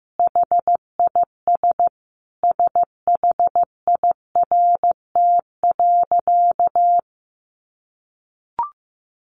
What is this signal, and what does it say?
Morse code, heard as a keyed tone, about 15 wpm, 700 Hz. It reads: HIS SHIRT.
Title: Additional Easy Phrases 15wpm